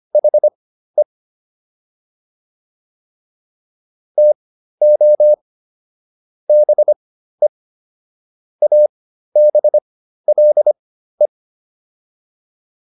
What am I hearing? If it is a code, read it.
Morse code, heard as a keyed tone, about 25 wpm, spaced as for 13 wpm, 600 Hz. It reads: HE TO BE ABLE